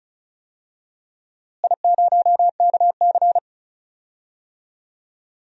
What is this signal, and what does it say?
Morse code, heard as a keyed tone, about 36 wpm, 700 Hz. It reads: I0KC
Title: DXCC-100 Call Signs - 4 Characters - Repeated 2x - 35wpm